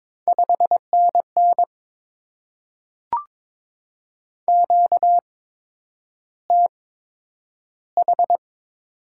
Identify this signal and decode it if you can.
Morse code, heard as a keyed tone, about 22 wpm, spaced as for 9 wpm, 700 Hz. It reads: 5NN QTH
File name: Top 20 QSO Elements - 8x Character Spacing - 22wpm